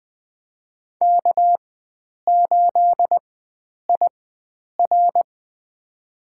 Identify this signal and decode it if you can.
Morse code, heard as a keyed tone, about 20 wpm, spaced as for 9 wpm, 700 Hz. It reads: K8IR
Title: USA Call Signs - 4 Characters - 4x Character Spacing - 20wpm